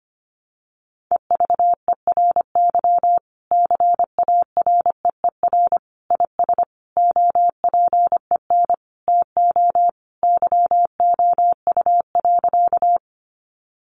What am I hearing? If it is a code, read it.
Morse code, heard as a keyed tone, about 25 wpm, 700 Hz. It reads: EVERY CAREER IS OPEN TO YOU.
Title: Sentences from Top 1000 Words 25wpm